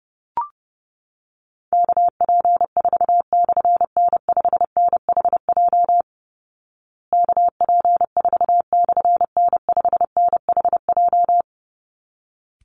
Morse code, heard as a keyed tone, about 30 wpm, 700 Hz. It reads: KP4/N5NHJ KP4/N5NHJ
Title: Call Signs Repeated 3x - 30wpm